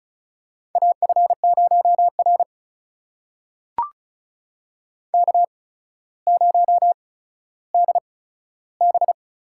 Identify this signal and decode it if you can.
Morse code, heard as a keyed tone, about 36 wpm, spaced as for 15 wpm, 700 Hz. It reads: AF0R K0DB
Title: USA Call Signs - 4 Characters - 8x Character Spacing - 35wpm